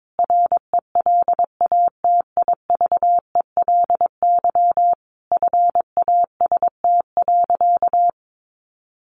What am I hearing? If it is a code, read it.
Morse code, heard as a keyed tone, about 22 wpm, 700 Hz. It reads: RELATIVELY FAST.
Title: Sentences from Top 1100 Words 22wpm